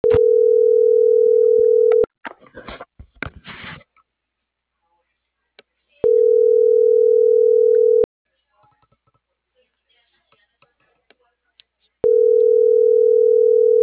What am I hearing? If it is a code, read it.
Ringback tone (North America), 440 + 480 Hz.
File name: CENTRAL MAERSK / out yunuel.morales - .34851